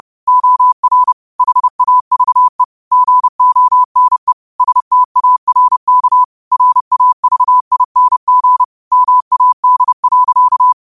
Morse code, heard as a keyed tone, about 30 wpm, 1000 Hz. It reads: OR HAVE GONE STARK RAVING MAD.